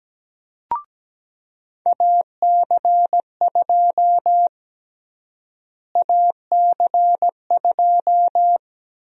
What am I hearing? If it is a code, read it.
Morse code, heard as a keyed tone, about 17 wpm, 700 Hz. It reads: AC2 AC2
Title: USA Call Sign Prefixes - Repeated 2x - 17wpm